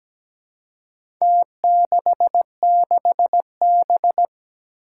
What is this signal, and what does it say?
Morse code, heard as a keyed tone, about 17 wpm, 700 Hz. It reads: T66B
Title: Four Challenging Letter-Number Combinations 17wpm